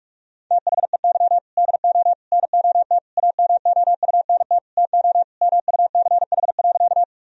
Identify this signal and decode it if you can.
Morse code, heard as a keyed tone, about 42 wpm, 700 Hz. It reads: THEY DO NOT AMOUNT TO MUCH.
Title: Sentences from Top 1000 Words - Review Part 4 - 45wpm